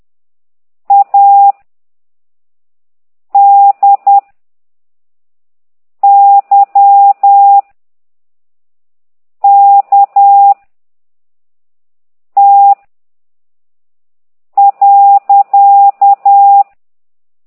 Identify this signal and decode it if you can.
Morse code, heard as a keyed tone, about 10 wpm, 800 Hz. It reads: A D Y K T .